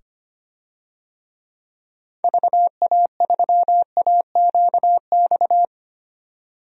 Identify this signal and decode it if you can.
Morse code, heard as a keyed tone, about 25 wpm, 700 Hz. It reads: VA3AQX
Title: DXCC-100 Call Signs - 6 Characters - 25wpm